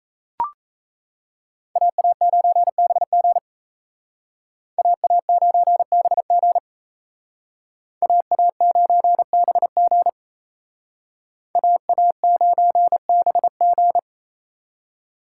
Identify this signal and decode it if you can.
Morse code, heard as a keyed tone, about 36 wpm, 700 Hz. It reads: AA9BG AA9BG AA9BG AA9BG